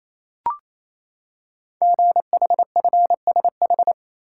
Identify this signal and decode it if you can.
Morse code, heard as a keyed tone, about 28 wpm, 700 Hz. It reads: GHFSH
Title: Code Groups - 5 Letters - 28wpm